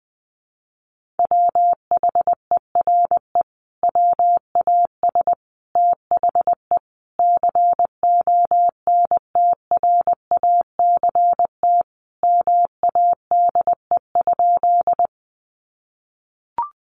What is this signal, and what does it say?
Morse code, heard as a keyed tone, about 20 wpm, 700 Hz. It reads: WHERE WAS THE CONTRACT MADE?
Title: Sentences from Top 1400 Words 20wpm